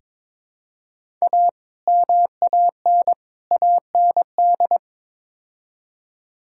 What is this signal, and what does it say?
Morse code, heard as a keyed tone, about 22 wpm, 700 Hz. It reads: A MAN AND
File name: practice / 22wpm / Build A Phrase - 5-Word Phrases - 22wpm